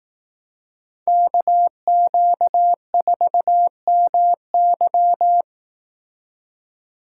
Morse code, heard as a keyed tone, about 18 wpm, 700 Hz. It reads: KQ4MY
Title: Speed Racing - USA 2x2 Call Signs 15wpm